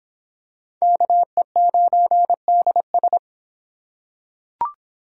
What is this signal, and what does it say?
Morse code, heard as a keyed tone, about 24 wpm, 700 Hz. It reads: KE9DS